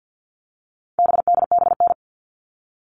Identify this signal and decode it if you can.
Morse code, heard as a keyed tone, about 50 wpm, 700 Hz. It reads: BDBN
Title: All Cut Numbers - 4-Digit 50wpm